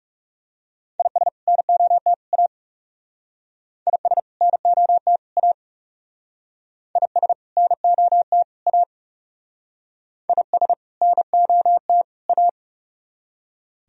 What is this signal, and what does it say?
Morse code, heard as a keyed tone, about 38 wpm, 690 Hz. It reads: IS NOT A IS NOT A IS NOT A IS NOT A